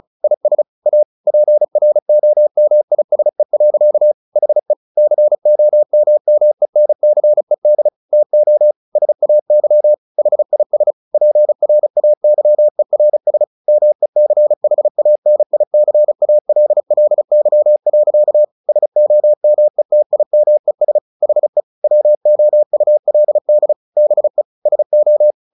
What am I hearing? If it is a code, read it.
Morse code, heard as a keyed tone, about 34 wpm, 600 Hz. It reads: IS A PROMISE. HE COMMENCED TO SAY HIS PRAYERS MECHANICALLY. SOMETIMES HE WOULD BE SO
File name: the old man and the sea / the old man and the sea 1 35wpm